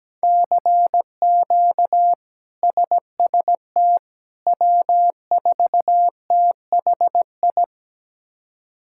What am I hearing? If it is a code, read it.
Morse code, heard as a keyed tone, about 17 wpm, 700 Hz. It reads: CQ SST W4THI
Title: K1USN SST QSO Practice - 17wpm